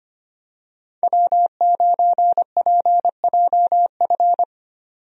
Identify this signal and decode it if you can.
Morse code, heard as a keyed tone, about 25 wpm, 700 Hz. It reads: W9PJF
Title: USA Call Signs - 5 Characters - 25wpm